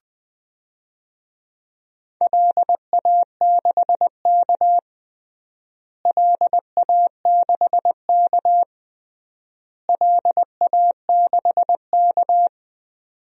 Morse code, heard as a keyed tone, about 20 wpm, 700 Hz. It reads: LA6K LA6K LA6K
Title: DXCC-100 Call Signs - 4 Characters - Repeated 3x - 20wpm